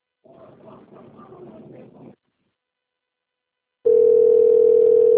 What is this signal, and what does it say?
Ringback tone (North America), 440 + 480 Hz.